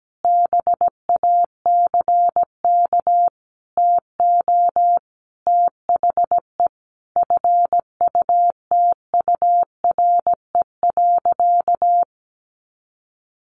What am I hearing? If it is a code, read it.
Morse code, heard as a keyed tone, about 17 wpm, 700 Hz. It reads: BACK TO THE FUTURE.
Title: A Sound Of Thunder 17wpm